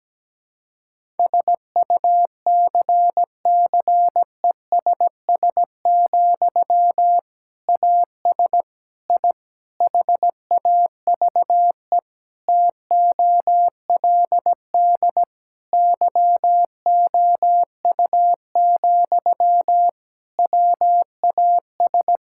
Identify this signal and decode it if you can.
Morse code, heard as a keyed tone, about 17 wpm, 700 Hz. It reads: SUCCESS, AS I HAVE TOLD YOU, WAS
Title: Sentences from Top 1000 Words - Review Part 6 - 17wpm